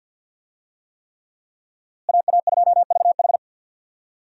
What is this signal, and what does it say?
Morse code, heard as a keyed tone, about 51 wpm, 700 Hz. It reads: AA2VH
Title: USA Call Signs - 5 Characters - Repeated 3x - 50wpm